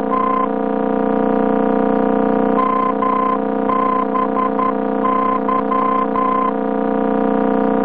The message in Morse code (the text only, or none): none